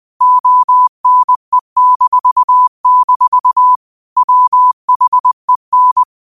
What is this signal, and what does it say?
Morse code, heard as a keyed tone, about 20 wpm, 1000 Hz. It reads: ONE-- WHEN
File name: qotdfiles / 20wpm